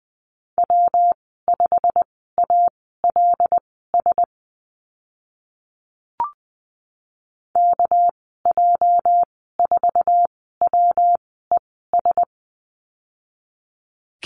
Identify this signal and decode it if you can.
Morse code, heard as a keyed tone, about 20 wpm, spaced as for 15 wpm, 700 Hz. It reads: W5ALS KJ4WES
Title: 219 Lesson 35g - Number 0 - Call Signs 20wpm